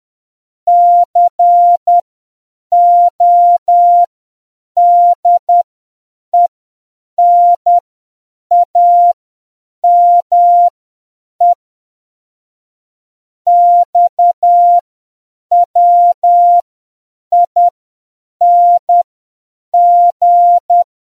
Morse code, heard as a keyed tone, about 10 wpm, spaced as for 7 wpm, 700 Hz. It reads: CODENAME XWING